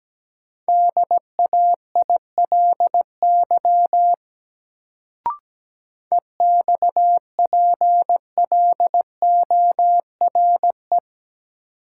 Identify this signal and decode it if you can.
Morse code, heard as a keyed tone, about 17 wpm, 700 Hz. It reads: DAILY EXPLORE
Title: Sentences from Top 1100 Words - New Word Drill - 17wpm